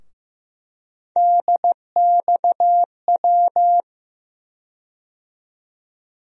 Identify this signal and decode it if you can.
Morse code, heard as a keyed tone, about 15 wpm, 700 Hz. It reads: DXW
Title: USA Call Sign Postfixes - 3 Letters 15wpm